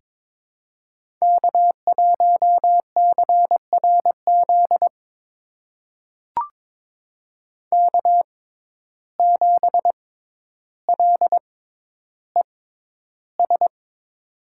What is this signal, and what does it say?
Morse code, heard as a keyed tone, about 22 wpm, spaced as for 11 wpm, 700 Hz. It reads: K1CRZ K7LES
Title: USA Call Signs - 6x Character Spacing - 22wpm